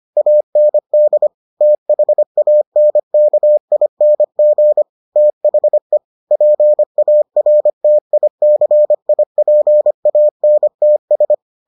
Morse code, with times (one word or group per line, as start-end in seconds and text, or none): 0.2-1.3 AND
1.6-4.8 THANKING
5.2-6.0 THE
6.3-11.3 PARTICIPANTS